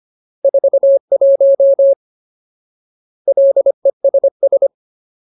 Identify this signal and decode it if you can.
Morse code, heard as a keyed tone, about 25 wpm, 550 Hz. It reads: 41 LESS
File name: Ultimate Word List / 25WPM Ultimate